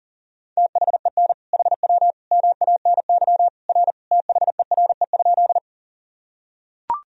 Morse code, heard as a keyed tone, about 40 wpm, 700 Hz. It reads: THEN HW MANY R THERE?
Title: Sentences from Top 100 Words and Abbreviations 40wpm